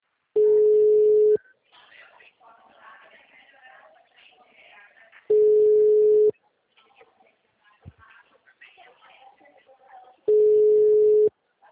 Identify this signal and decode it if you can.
Ringback tone (Europe, 425 Hz), with speech.